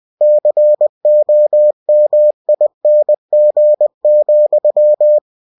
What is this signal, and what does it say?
Morse code, heard as a keyed tone, about 20 wpm, 600 Hz. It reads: COMING,
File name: winnie-20wpm